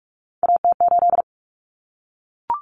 Morse code, heard as a keyed tone, about 44 wpm, 700 Hz. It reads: AT8
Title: Common Cut Numbers - 3-Digit 45wpm